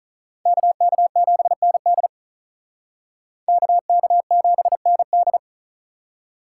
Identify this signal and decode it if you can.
Morse code, heard as a keyed tone, about 45 wpm, 700 Hz. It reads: KK7ND KK7ND